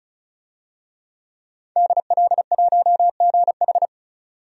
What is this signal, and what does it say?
Morse code, heard as a keyed tone, about 34 wpm, 700 Hz. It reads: DL1GH
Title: DXCC-100 Call Signs - 5 Characters - 35wpm